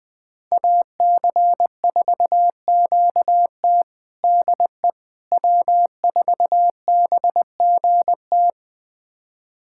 Morse code, heard as a keyed tone, about 20 wpm, 700 Hz. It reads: AC4QT DE W4BGT
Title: USA Callsign DE Callsign - 5 Characters - 20wpm